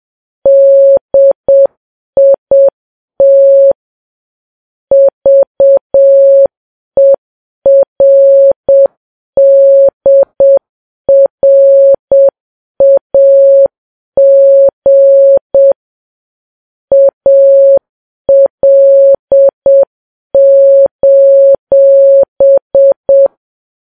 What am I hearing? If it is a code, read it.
Morse code, heard as a keyed tone, about 7 wpm, 550 Hz. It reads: DIT VERDRAG AL:
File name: Files Web tls / Morse on9fv / morse7kps